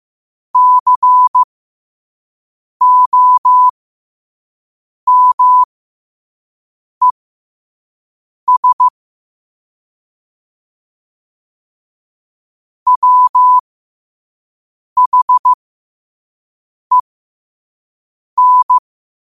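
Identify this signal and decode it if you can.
Morse code, heard as a keyed tone, about 15 wpm, spaced as for 5 wpm, 1000 Hz. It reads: COMES WHEN
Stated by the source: Morse code sent at 5 wpm.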